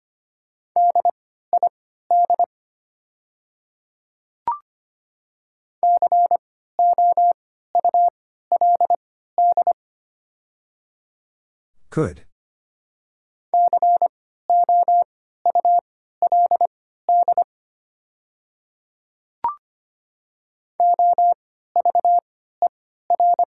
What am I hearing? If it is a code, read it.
Morse code, heard as a keyed tone, about 25 wpm, spaced as for 14 wpm, 700 Hz. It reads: DID COULD COULD OVER
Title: 3x Character Spacing - Top 100 Words - 25wpm